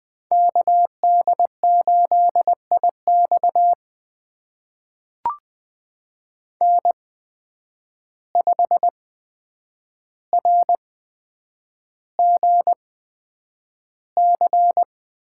Morse code, heard as a keyed tone, about 20 wpm, spaced as for 9 wpm, 700 Hz. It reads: KD8IX N5RGC
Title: USA Call Signs - 5 Characters - 8x Character Spacing - 20wpm